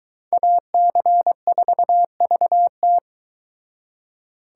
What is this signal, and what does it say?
Morse code, heard as a keyed tone, about 22 wpm, 700 Hz. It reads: AC4VT